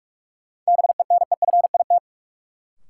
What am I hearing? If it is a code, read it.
Morse code, heard as a keyed tone, about 46 wpm, 700 Hz. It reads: BENEFIT
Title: Sentences from Top 1300 Words - New Word Drill Alt - 45wpm